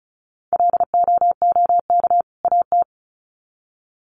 Morse code, heard as a keyed tone, about 34 wpm, 700 Hz. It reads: LOOK AT
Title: Sets of 2 Words 35wpm